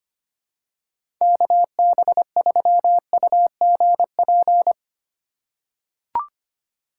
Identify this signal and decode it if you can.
Morse code, heard as a keyed tone, about 25 wpm, 700 Hz. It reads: KB3UGP
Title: USA Call Signs - 6 Characters - 3x Character Spacing - 25wpm